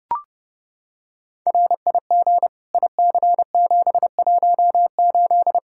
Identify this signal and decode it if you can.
Morse code, heard as a keyed tone, about 32 wpm, 700 Hz. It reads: RIG IC718
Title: Practice QSO - Exchange 2 - Info - 30wpm